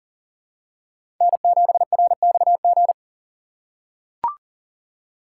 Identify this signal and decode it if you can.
Morse code, heard as a keyed tone, about 40 wpm, 700 Hz. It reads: N7RXG
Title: USA Call Signs - 5 Characters - 2x Character Spacing - 40wpm